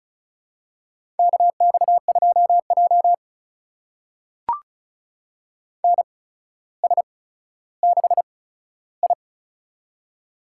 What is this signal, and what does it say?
Morse code, heard as a keyed tone, about 36 wpm, spaced as for 15 wpm, 700 Hz. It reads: KX2J NS6I
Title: USA Call Signs - 4 Characters - 8x Character Spacing - 35wpm